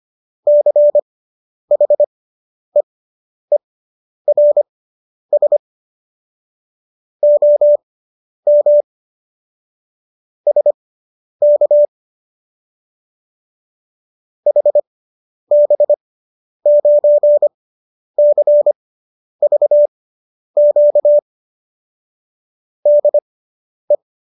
Morse code, heard as a keyed tone, about 25 wpm, spaced as for 10 wpm, 590 Hz. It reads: CHEERS OM SK HB9CVQ DE